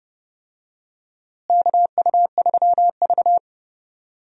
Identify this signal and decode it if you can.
Morse code, heard as a keyed tone, about 29 wpm, 700 Hz. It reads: KU3V
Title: USA Call Signs - 4 Characters - Repeated 2x - 30wpm